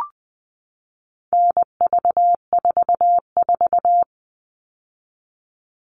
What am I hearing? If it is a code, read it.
Morse code, heard as a keyed tone, about 20 wpm, 700 Hz. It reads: NV44
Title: All Cut Numbers - 4-Digit 20wpm